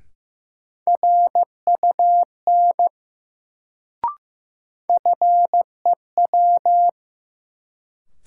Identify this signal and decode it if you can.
Morse code, heard as a keyed tone, about 15 wpm, 700 Hz. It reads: RUN FEW
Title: Sentences from Top 300 Words - New Word Drill - 15wpm